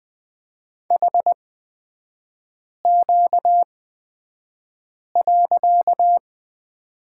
Morse code, heard as a keyed tone, about 20 wpm, spaced as for 5 wpm, 700 Hz. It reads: HQ.
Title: 255 Lesson 41e - Period - Three Characters 20wpm